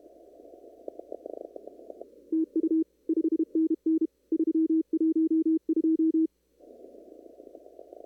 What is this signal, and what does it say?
Morse code, heard as a keyed tone, about 32 wpm, 320 Hz. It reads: TU 5NN 312
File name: CW / kt5j wpx cw-op-k5pi